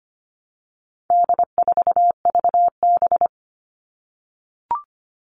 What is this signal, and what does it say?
Morse code, heard as a keyed tone, about 25 wpm, 700 Hz. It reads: D4VB
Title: All Cut Numbers - 4-Digit 25wpm